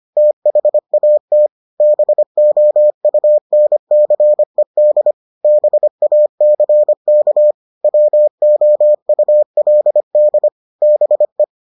Morse code, heard as a keyed tone, about 25 wpm, 600 Hz. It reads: THAT BOUNCED BACK WOULD BE